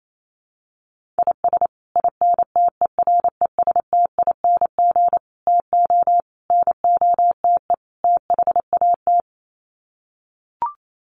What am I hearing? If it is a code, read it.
Morse code, heard as a keyed tone, about 28 wpm, 700 Hz. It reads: IS INTERESTING TO NOTE THAT
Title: Sets of 5 Words Encore2 28wpm